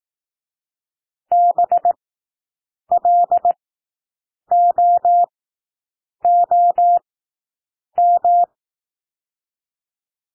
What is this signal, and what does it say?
Morse code, heard as a keyed tone, about 18 wpm, spaced as for 7 wpm, 700 Hz. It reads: BLOOM
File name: SOUND WORDS 18 wpm